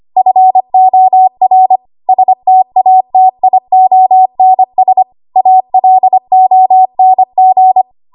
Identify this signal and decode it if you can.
Morse code, heard as a keyed tone, about 25 wpm, 750 Hz. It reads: FOR STATIONS ALONG
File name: Morse / 25WPM